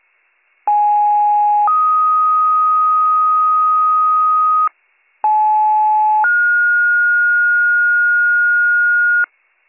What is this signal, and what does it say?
Two-tone page, heard as a one-second tone, then a three-second tone, about 830 then 1250 Hz, about 860 then 1430 Hz, with noise. It is heard from 0.7 to 9.2 s.